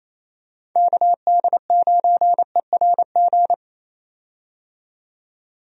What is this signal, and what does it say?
Morse code, heard as a keyed tone, about 28 wpm, 700 Hz. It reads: KD9ERG
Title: USA Call Signs - 6 Characters - 28wpm